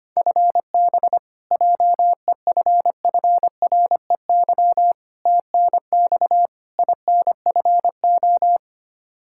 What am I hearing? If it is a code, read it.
Morse code, heard as a keyed tone, about 25 wpm, 700 Hz. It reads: FB JEFFREY TNX INFO
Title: K1USN SST QSO Practice 25wpm